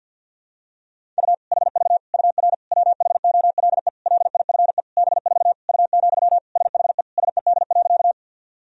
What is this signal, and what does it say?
Morse code, heard as a keyed tone, about 53 wpm, 700 Hz. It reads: U HV UR WHOLE LIFE B4 U, SHE SED.
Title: Sentences from Top 400 Words and Abbreviations 50wpm